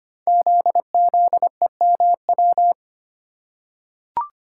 Morse code, heard as a keyed tone, about 25 wpm, 700 Hz. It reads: ZZEMW
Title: Code Groups - 5 Letters - 25wpm